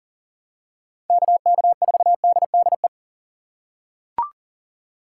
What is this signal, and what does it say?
Morse code, heard as a keyed tone, about 40 wpm, 700 Hz. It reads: KK4DDE
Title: USA Call Signs - 6 Characters - 8x Character Spacing - 40wpm